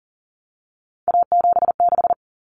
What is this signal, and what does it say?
Morse code, heard as a keyed tone, about 40 wpm, 700 Hz. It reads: A76
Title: Common Cut Numbers - 3-Digit 40wpm